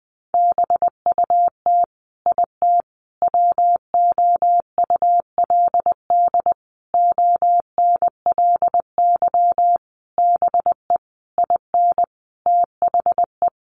Morse code, heard as a keyed tone, about 19 wpm, 700 Hz. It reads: BUT IT WOULD ONLY BE IN THE
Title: Sentences from Top 1500 Words 20wpm